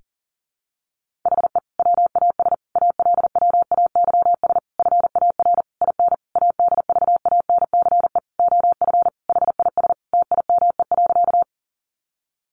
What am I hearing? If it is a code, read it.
Morse code, heard as a keyed tone, about 37 wpm, 700 Hz. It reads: HE WAS ALWAYS FAR IN ADVANCE OF HIS TIME.
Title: Sentences from Top 1400 Words 40wpm